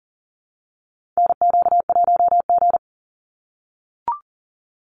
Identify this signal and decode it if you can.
Morse code, heard as a keyed tone, about 40 wpm, 700 Hz. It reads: NQ1G
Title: USA Call Signs 40wpm